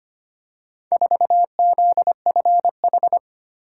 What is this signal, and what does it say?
Morse code, heard as a keyed tone, about 25 wpm, 700 Hz. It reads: 4ZFH
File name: practice / 25wpm / Four Letter-Number Combinations - 25wpm